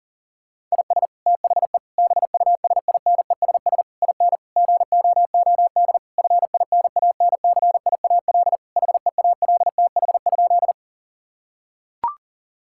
Morse code, heard as a keyed tone, about 40 wpm, 700 Hz. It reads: IS THE BUSINESS IN GOOD FINANCIAL HEALTH?
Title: Sentences from Top 1000 Words - Review Part 5 - 40wpm